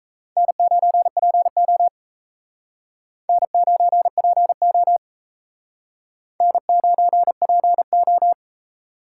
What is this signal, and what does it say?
Morse code, heard as a keyed tone, about 34 wpm, 700 Hz. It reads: N9PO N9PO N9PO
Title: Speed Racing - USA 1x2 Call Signs 28wpm